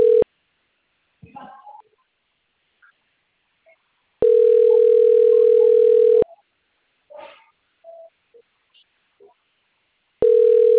Ringback tone (North America), 440 + 480 Hz.